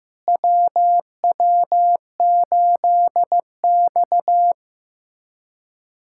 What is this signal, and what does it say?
Morse code, heard as a keyed tone, about 16 wpm, 700 Hz. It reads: WW8X
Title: USA Callsign DE Callsign - 4 Characters - 15wpm